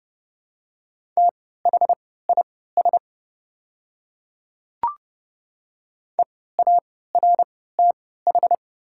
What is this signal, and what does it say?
Morse code, heard as a keyed tone, about 30 wpm, spaced as for 17 wpm, 700 Hz. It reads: THIS EARTH
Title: 3x Character Spacing - Top 200 Words - 30wpm